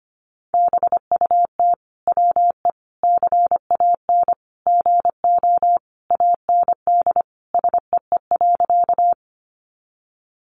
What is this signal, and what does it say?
Morse code, heard as a keyed tone, about 25 wpm, 700 Hz. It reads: BUT WE CAN GO AND SEE.
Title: Sentences from Top 100 Words 25wpm